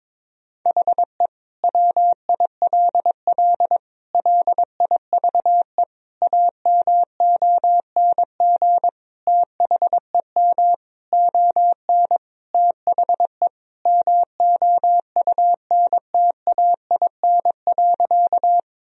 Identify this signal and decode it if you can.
Morse code, heard as a keyed tone, about 22 wpm, 700 Hz. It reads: HE WILL LIVE AMONG THEM ON THE MOUNTAIN.
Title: Sentences from Top 1000 Words - Review Part 3 - 22wpm